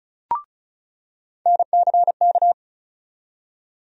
Morse code, heard as a keyed tone, about 36 wpm, 700 Hz. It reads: NCK
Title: USA Call Sign Postfixes - 3 Letters 35wpm